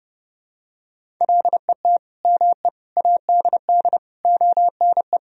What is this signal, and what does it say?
Morse code, heard as a keyed tone, about 30 wpm, 700 Hz. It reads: LET ME ADD ONE
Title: Sentences from Top 1000 Words - Review Part 1 - 30wpm